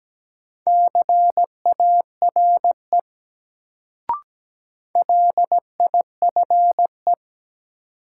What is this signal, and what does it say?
Morse code, heard as a keyed tone, about 17 wpm, 700 Hz. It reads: CARE LIFE
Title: Sentences from Top 300 Words - New Word Drill - 17wpm